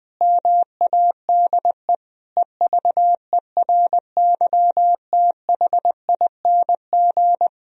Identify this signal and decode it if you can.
Morse code, heard as a keyed tone, about 20 wpm, 700 Hz. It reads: MADE EVERYTHING
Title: Sentences from Top 1000 Words - Review Part 3 - 20wpm